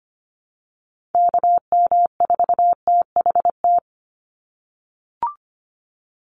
Morse code, heard as a keyed tone, about 25 wpm, 700 Hz. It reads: KM4THT
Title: USA Call Signs 25wpm